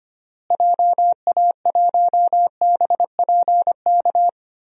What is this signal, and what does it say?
Morse code, heard as a keyed tone, about 25 wpm, 700 Hz. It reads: JA1BPK
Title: DXCC-100 Call Signs - 6 Characters - Repeated 3x - 25wpm